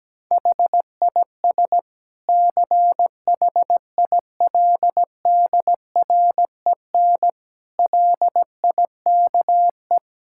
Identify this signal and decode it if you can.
Morse code, heard as a keyed tone, about 17 wpm, 700 Hz. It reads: HIS CHILDREN LIKE
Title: Sentences from Top 1600 Words 17wpm